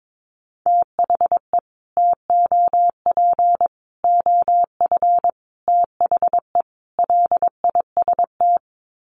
Morse code, heard as a keyed tone, about 22 wpm, 700 Hz. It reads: THE TOP OF THE LIST
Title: Sets of 5 Words Encore2 22wpm